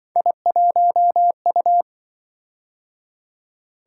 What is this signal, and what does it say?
Morse code, heard as a keyed tone, about 25 wpm, 700 Hz. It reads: I1U
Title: Speed-Racing - Three Letter-Number Combinations - 20wpm